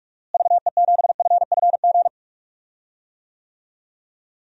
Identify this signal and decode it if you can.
Morse code, heard as a keyed tone, about 43 wpm, 700 Hz. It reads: VE7FFG
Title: DXCC-100 Call Signs - 6 Characters - Repeated 2x - 45wpm